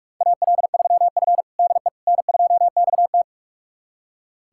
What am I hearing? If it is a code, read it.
Morse code, heard as a keyed tone, about 44 wpm, 690 Hz. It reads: AL3F DE N2XT
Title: USA Callsign DE Callsign - 4 Characters - 45wpm